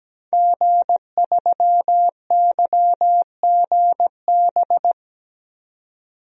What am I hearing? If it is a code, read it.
Morse code, heard as a keyed tone, about 17 wpm, 700 Hz. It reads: G3YGB
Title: DXCC-100 Call Signs - 5 Characters - 17wpm